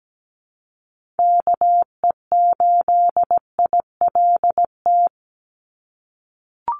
Morse code, heard as a keyed tone, about 17 wpm, 700 Hz. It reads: KE8ILT
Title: USA Call Signs 17wpm